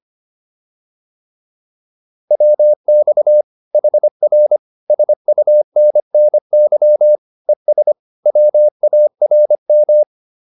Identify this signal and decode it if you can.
Morse code, heard as a keyed tone, about 25 wpm, 600 Hz. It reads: WX HR SUNNY ES WARM
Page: advanced-practice-files